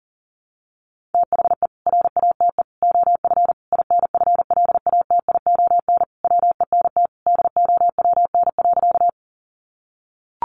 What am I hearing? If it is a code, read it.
Morse code, heard as a keyed tone, about 40 wpm, 700 Hz. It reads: THE RATE OF INFLATION WENT DOWN.
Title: Sentences from Top 1100 Words 40wpm